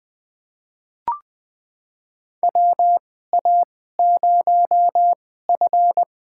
Morse code, heard as a keyed tone, about 20 wpm, spaced as for 15 wpm, 700 Hz. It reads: WA0F
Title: USA Call Signs - 4 Characters - 2x Character Spacing - 20wpm